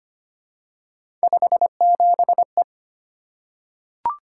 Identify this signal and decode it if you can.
Morse code, heard as a keyed tone, about 25 wpm, 700 Hz. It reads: 57E
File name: Three Letter-Number Combinations - 25wpm